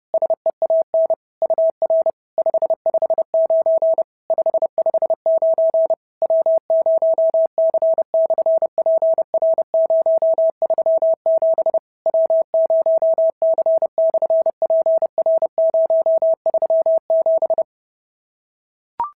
Morse code, heard as a keyed tone, about 30 wpm, 650 Hz. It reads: SEAN UR 559 559 W0C/PR037 W0C/PR037
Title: SOTA QSO Practice 30wpm